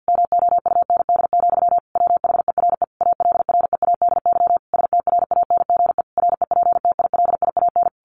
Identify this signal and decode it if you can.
Morse code, heard as a keyed tone, about 50 wpm, 700 Hz. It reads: MOUND, WHERE ALREADY STRANGE REPTILIAN
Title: A Sound Of Thunder 50wpm